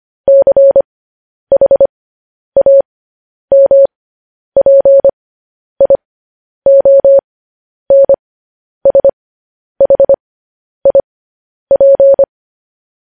Morse code, heard as a keyed tone, about 25 wpm, spaced as for 10 wpm, 550 Hz. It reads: CHAMPIONSHIP
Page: News Headlines in Morse Code at 10 WPM